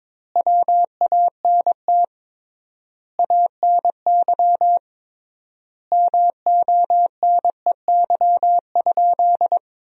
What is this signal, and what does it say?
Morse code, heard as a keyed tone, about 22 wpm, 700 Hz. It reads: WANT ANY MONEY?
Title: 3x Word-Spacing - Sentences from Top 400 Words - 22wpm